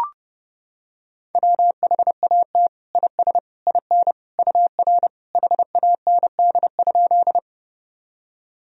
Morse code, heard as a keyed tone, about 30 wpm, 700 Hz. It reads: WHAT IS IN UR HAND?
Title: Sentences from Top 200 Words and Abbreviations 30wpm